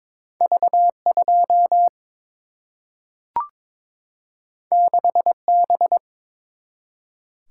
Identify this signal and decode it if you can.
Morse code, heard as a keyed tone, about 22 wpm, 700 Hz. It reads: V 2 6 B
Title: Single-Letter Single-Number 22wpm